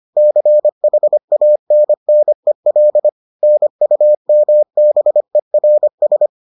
Morse code, heard as a keyed tone, about 25 wpm, 600 Hz. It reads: CHANNEL NUMBERS